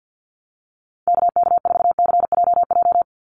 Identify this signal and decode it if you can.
Morse code, heard as a keyed tone, about 50 wpm, 700 Hz. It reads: KK4CJJ
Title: USA Call Signs 50wpm